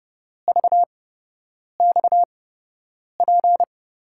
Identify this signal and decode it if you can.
Morse code, heard as a keyed tone, about 30 wpm, spaced as for 8 wpm, 700 Hz. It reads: VXP
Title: USA Call Signs - 8x Character Spacing - 30wpm